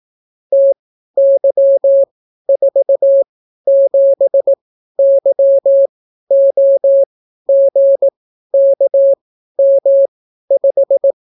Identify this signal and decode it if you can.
Morse code, heard as a keyed tone, about 18 wpm, spaced as for 12 wpm, 550 Hz. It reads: TY47YOGKM5